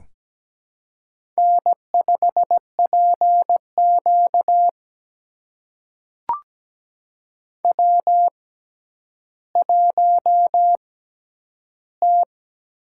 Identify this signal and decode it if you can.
Morse code, heard as a keyed tone, about 17 wpm, spaced as for 10 wpm, 700 Hz. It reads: N5PQ W1T
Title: USA Call Signs - 4 Characters - 6x Character Spacing - 17wpm